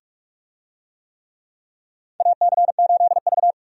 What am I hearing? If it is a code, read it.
Morse code, heard as a keyed tone, about 46 wpm, 700 Hz. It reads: AC8V
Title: USA Call Signs - 4 Characters - 45wpm